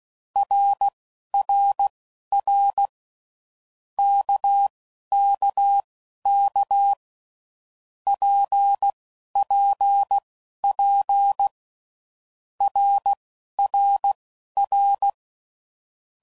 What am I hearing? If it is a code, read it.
Morse code, heard as a keyed tone, about 16 wpm, spaced as for 11 wpm, 800 Hz. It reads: RRR KKK PPP RRR
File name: Groups / Basic Letter Groups / RKPX